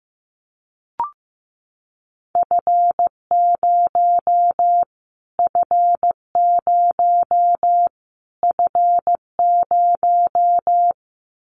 Morse code, heard as a keyed tone, about 15 wpm, 700 Hz. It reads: F0 F0 F0
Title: Callsign Prefixes for Top 25 DXCC - Repeated 3x - 15wpm